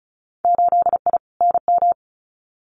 Morse code, heard as a keyed tone, about 34 wpm, 700 Hz. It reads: 8I NM
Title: Winter Field Day Exchanges Encore 35wpm